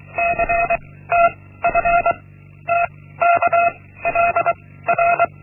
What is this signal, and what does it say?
Morse code, heard as a keyed tone, about 23 wpm, 690 Hz. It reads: C T F T K L R
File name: Linksmakalnis-1